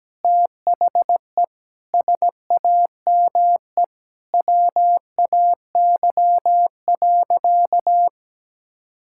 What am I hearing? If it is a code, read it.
Morse code, heard as a keyed tone, about 17 wpm, 700 Hz. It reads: THE SAME WAY.
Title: Sentences from Top 1600 Words 17wpm